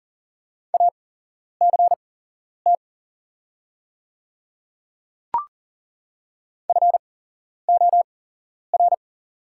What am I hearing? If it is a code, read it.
Morse code, heard as a keyed tone, about 38 wpm, spaced as for 11 wpm, 700 Hz. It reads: ACT FOR
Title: 8x Character Spacing - Top 200 Words - 40wpm